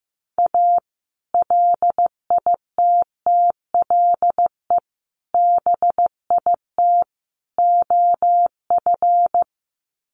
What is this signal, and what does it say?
Morse code, heard as a keyed tone, about 15 wpm, 700 Hz. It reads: A LITTLE BIT OF
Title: Sets of 4 Words 15wpm